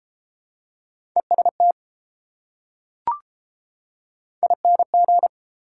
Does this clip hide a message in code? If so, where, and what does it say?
Yes: Morse code from 1.2 to 5.3 s, heard as a keyed tone, about 29 wpm, 700 Hz. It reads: EST ING